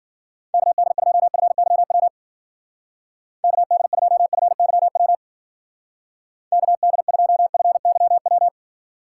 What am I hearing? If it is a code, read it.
Morse code, heard as a keyed tone, about 50 wpm, 700 Hz. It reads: KD2FYW KD2FYW KD2FYW
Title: Speed Racing - USA 2x3 Call Signs 40wpm